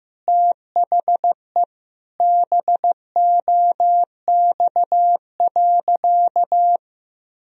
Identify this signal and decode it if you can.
Morse code, heard as a keyed tone, about 15 wpm, 700 Hz. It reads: THE BOX.
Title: Sentences from Top 1000 Words - Review Part 2 - 15wpm